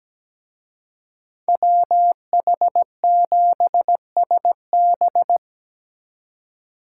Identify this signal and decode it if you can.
Morse code, heard as a keyed tone, about 20 wpm, 700 Hz. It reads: WH7SB
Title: Speed Racing - USA 2x2 Call Signs 17wpm